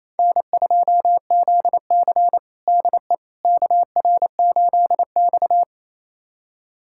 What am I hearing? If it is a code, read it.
Morse code, heard as a keyed tone, about 27 wpm, 700 Hz. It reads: N2ZC DE KR8X
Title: USA Callsign DE Callsign - 4 Characters - 28wpm